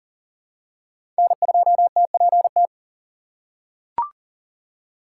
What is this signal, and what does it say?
Morse code, heard as a keyed tone, about 40 wpm, 700 Hz. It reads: N2TPT
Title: USA Call Signs - 4x Character Spacing - 40wpm